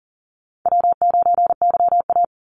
Morse code, heard as a keyed tone, about 38 wpm, 700 Hz. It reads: W9YA
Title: Call Signs Repeated 2x - 40wpm